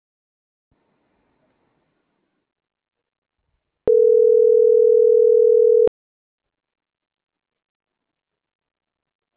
Ringback tone (North America), 440 + 480 Hz.